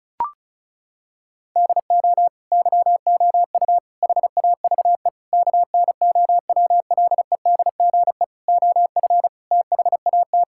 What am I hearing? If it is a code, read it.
Morse code, heard as a keyed tone, about 36 wpm, 700 Hz. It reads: DO YOU HAVE KNOWLEDGE OF THAT
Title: Sentences from Top 1000 Words - Review Part 6 - 35wpm